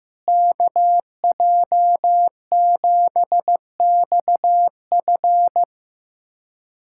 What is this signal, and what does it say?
Morse code, heard as a keyed tone, about 15 wpm, 700 Hz. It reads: KJ7XF
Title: USA Callsign DE Callsign - 5 Characters - 15wpm